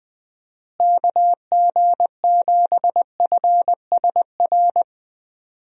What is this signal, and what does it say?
Morse code, heard as a keyed tone, about 21 wpm, 700 Hz. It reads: KG7FSR
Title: USA Call Signs - 6 Characters - Repeated 2x - 20wpm